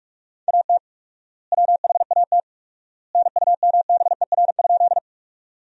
Morse code, heard as a keyed tone, about 46 wpm, 700 Hz. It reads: AT WHAT NUMBER?
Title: 4x Word-Spacing - Sentences from Top 100 Words - 45wpm